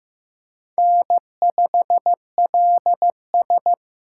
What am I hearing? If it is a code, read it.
Morse code, heard as a keyed tone, about 15 wpm, 700 Hz. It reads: N5LS
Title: USA Call Signs - 4 Characters - 8x Character Spacing - 15wpm